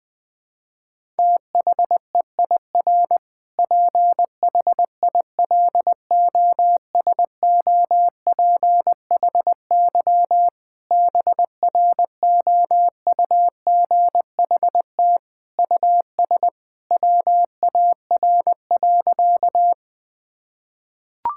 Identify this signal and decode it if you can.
Morse code, heard as a keyed tone, about 21 wpm, 700 Hz. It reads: THEIR PHILOSOPHY BROUGHT US WAR.
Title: Sentences from Top 1000 Words - Review Part 6 - 20wpm